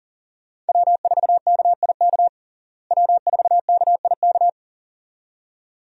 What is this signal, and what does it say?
Morse code, heard as a keyed tone, about 40 wpm, 700 Hz. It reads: W4KIK W4KIK
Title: USA Call Signs - 5 Characters - Repeated 2x - 40wpm